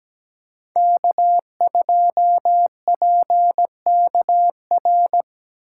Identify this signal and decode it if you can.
Morse code, heard as a keyed tone, about 20 wpm, 700 Hz. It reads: K2PKR
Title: Speed Racing - USA 1x3 Call Signs 17wpm